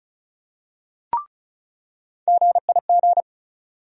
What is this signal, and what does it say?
Morse code, heard as a keyed tone, about 36 wpm, 700 Hz. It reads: GIG
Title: USA Call Sign Postfixes - 3 Letters 35wpm